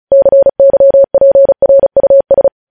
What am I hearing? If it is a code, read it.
Morse code, heard as a keyed tone, about 36 wpm, 550 Hz. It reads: CYPRUS
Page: News Headlines in Morse Code at 35 WPM